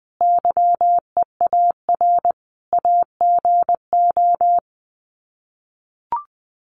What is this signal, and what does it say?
Morse code, heard as a keyed tone, about 20 wpm, 700 Hz. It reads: YEAR AGO
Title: Sets of 5 Words Encore2 20wpm